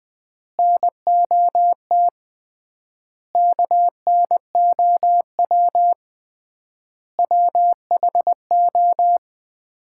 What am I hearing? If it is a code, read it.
Morse code, heard as a keyed tone, about 20 wpm, 700 Hz. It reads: NOT KNOW WHO
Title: 3x Word-Spacing - Sets of 5 Words - 20wpm